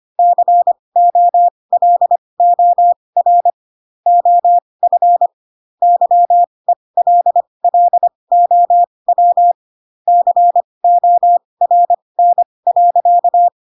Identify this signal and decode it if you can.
Morse code, heard as a keyed tone, about 25 wpm, spaced as for 20 wpm, 700 Hz. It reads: COLOR OF YELLOW CORN.
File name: Pinocchio / Pinocchio ch2 15 25 ik1hge